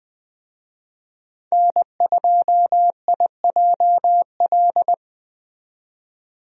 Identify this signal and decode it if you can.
Morse code, heard as a keyed tone, about 21 wpm, 700 Hz. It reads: N2IJL
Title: USA Call Signs - 5 Characters - 20wpm